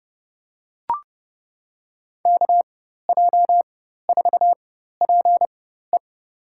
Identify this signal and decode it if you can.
Morse code, heard as a keyed tone, about 30 wpm, spaced as for 14 wpm, 700 Hz. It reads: KJ4PE
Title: USA Call Signs - 4x Character Spacing - 30wpm